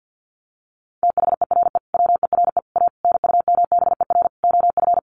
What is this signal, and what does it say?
Morse code, heard as a keyed tone, about 50 wpm, 700 Hz. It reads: THERE WERE A NUMBER OF
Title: Sets of 5 Words Encore2 50wpm